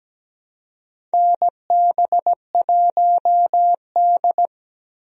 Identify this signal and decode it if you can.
Morse code, heard as a keyed tone, about 17 wpm, 700 Hz. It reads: NB1D
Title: USA Call Signs - 4 Characters - Repeated 3x - 17wpm